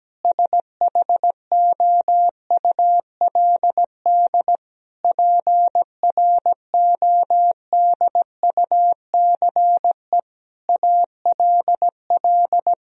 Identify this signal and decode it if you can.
Morse code, heard as a keyed tone, about 17 wpm, 700 Hz. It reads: SHOULD PRODUCE ALL
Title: Sentences from Top 1000 Words - Review Part 3 - 17wpm